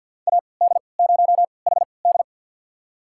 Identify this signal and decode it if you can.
Morse code, heard as a keyed tone, about 52 wpm, spaced as for 29 wpm, 700 Hz. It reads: AD0HD
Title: USA Call Signs - 5 Characters - 3x Character Spacing - 50wpm